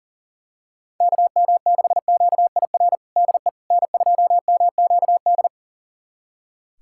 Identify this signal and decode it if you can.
Morse code, heard as a keyed tone, about 40 wpm, 700 Hz. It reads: KM6QIR DE N2MQD
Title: USA Callsign DE Callsign 40wpm